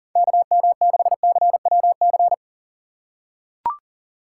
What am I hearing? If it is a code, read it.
Morse code, heard as a keyed tone, about 40 wpm, 700 Hz. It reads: KM6CWC
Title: USA Call Signs - 6 Characters - 2x Character Spacing - 40wpm